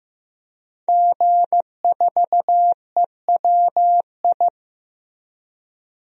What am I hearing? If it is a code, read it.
Morse code, heard as a keyed tone, about 15 wpm, 700 Hz. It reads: G4EWI
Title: DXCC-100 Call Signs - 5 Characters - Repeated 2x - 15wpm